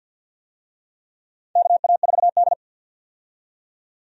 Morse code, heard as a keyed tone, about 53 wpm, 700 Hz. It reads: KA4D
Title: USA Call Signs - 4 Characters - 50wpm